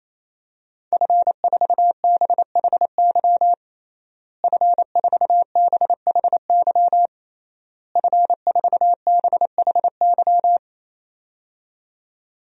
Morse code, heard as a keyed tone, about 28 wpm, 700 Hz. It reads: F4BHY F4BHY F4BHY
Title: DXCC-100 Call Signs - 5 Characters - Repeated 3x - 28wpm